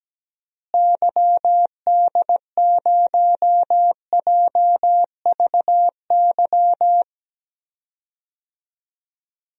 Morse code, heard as a keyed tone, about 17 wpm, 700 Hz. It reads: YD0JVY
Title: DXCC-100 Call Signs - 6 Characters - Repeated 2x - 17wpm